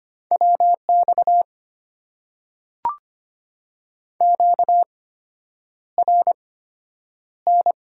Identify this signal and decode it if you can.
Morse code, heard as a keyed tone, about 25 wpm, spaced as for 9 wpm, 700 Hz. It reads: WX QRN
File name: Top 90 QSO Elements - 8x Character Spacing - 25wpm